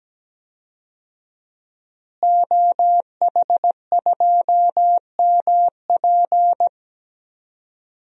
Morse code, heard as a keyed tone, about 17 wpm, 700 Hz. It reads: OH2MP
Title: DXCC-100 Call Signs - 5 Characters - Repeated 2x - 17wpm